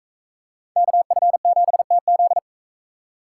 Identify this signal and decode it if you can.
Morse code, heard as a keyed tone, about 41 wpm, 700 Hz. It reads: KF7TZ